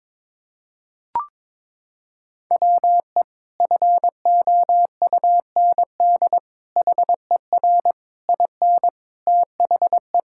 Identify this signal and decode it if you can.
Morse code, heard as a keyed tone, about 22 wpm, 700 Hz. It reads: WE FOUND HER IN THE
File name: practice / 22wpm / Sentences with 4-Word Phrases - 22wpm